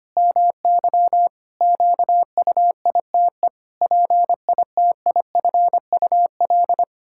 Morse code, heard as a keyed tone, about 25 wpm, 700 Hz. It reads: MY QUITE PITIFUL
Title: Pangram Challenge v2 - 25wpm